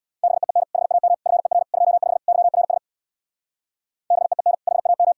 Morse code, heard as a keyed tone, about 62 wpm, 700 Hz. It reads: BEEN HAM FER 25 YRS BEEN HAM
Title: Speed-Racing Common QSO Phrases - 50wpm